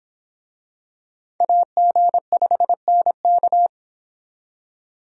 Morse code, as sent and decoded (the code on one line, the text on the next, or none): .- --. ..... -. -.-
AG5NK